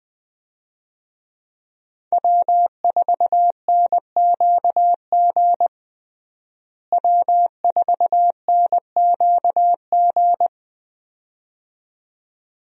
Morse code, heard as a keyed tone, about 21 wpm, 700 Hz. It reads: W4NQG W4NQG
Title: USA Call Signs - 5 Characters - Repeated 2x - 20wpm